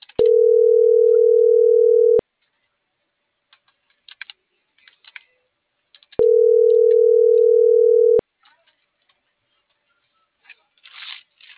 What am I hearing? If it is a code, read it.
Ringback tone (North America), 440 + 480 Hz.